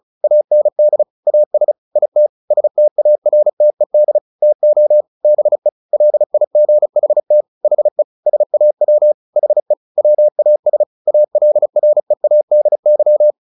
Morse code, heard as a keyed tone, about 34 wpm, 600 Hz. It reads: AND AS IT STARTED TO BE LIGHT HE SAW HE WAS ALREADY
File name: the old man and the sea / the old man and the sea 1 35wpm